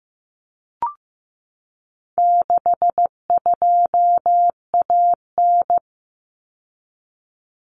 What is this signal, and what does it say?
Morse code, heard as a keyed tone, about 15 wpm, 700 Hz. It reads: 62AN
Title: Common Cut Numbers - 4-Digit 15wpm